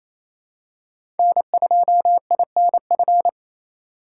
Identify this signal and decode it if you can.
Morse code, heard as a keyed tone, about 28 wpm, 700 Hz. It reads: N2INF
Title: USA Call Signs - 5 Characters - 2x Character Spacing - 28wpm